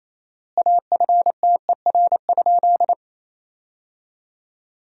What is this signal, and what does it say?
Morse code, heard as a keyed tone, about 28 wpm, 700 Hz. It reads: AFTER?
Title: 4x Word-Spacing - Sentences from Top 200 Words - 28wpm